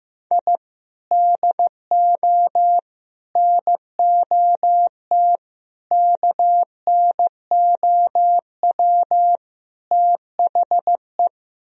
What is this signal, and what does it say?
Morse code, heard as a keyed tone, about 15 wpm, 700 Hz. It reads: I DO NOT KNOW THE
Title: Speed-Racing - Sets of 5 Words 15wpm